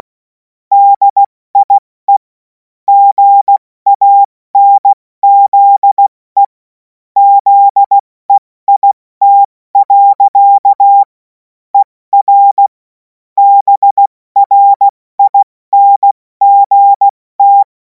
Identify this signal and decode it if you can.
Morse code, heard as a keyed tone, about 16 wpm, 800 Hz. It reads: DIE GANZE ZEIT. ER BRINGT
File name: Morsekurs / Witze / Witz 28. April 2026 70 F80